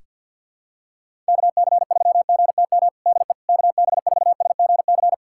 Morse code, heard as a keyed tone, about 47 wpm, 700 Hz. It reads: KC3GTM DE KB4SGC
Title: USA Callsign DE Callsign - 6 Characters - 50wpm